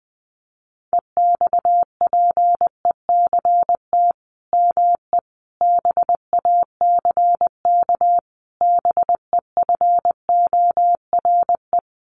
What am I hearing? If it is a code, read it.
Morse code, heard as a keyed tone, about 20 wpm, 700 Hz. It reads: EXPECT ME BACK BEFORE
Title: Sentences from Top 1100 Words 20wpm